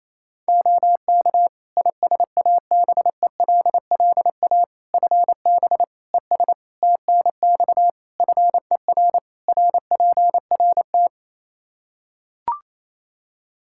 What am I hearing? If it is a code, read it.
Morse code, heard as a keyed tone, about 28 wpm, 700 Hz. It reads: OK ISABELLA FB ES TNX FER RPRT
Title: Practice QSO - Exchange 2 - Info - 28wpm